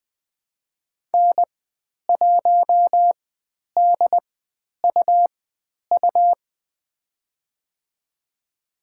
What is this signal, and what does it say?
Morse code, heard as a keyed tone, about 20 wpm, spaced as for 10 wpm, 700 Hz. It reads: N1DUU
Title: 102 Lesson 17g - Letter u - Call Signs 20wpm